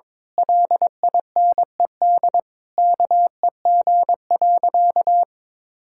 Morse code, heard as a keyed tone, about 22 wpm, 700 Hz. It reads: LINED KEG.
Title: Pangram Challenge v2 - 22wpm